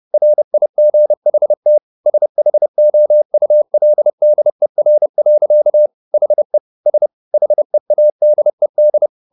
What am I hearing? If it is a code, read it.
Morse code, heard as a keyed tone, about 30 wpm, 600 Hz. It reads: RIGHT SHOULDER. HE S HEADED
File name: the old man and the sea / the old man and the sea 1 30wpm